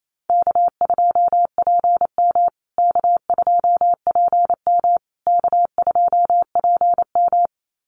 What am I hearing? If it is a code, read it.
Morse code, heard as a keyed tone, about 28 wpm, 700 Hz. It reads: K2PM K2PM K2PM
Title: USA Call Signs Repeated 3x 28wpm